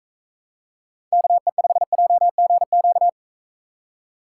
Morse code, heard as a keyed tone, about 45 wpm, 700 Hz. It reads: KE5JGQ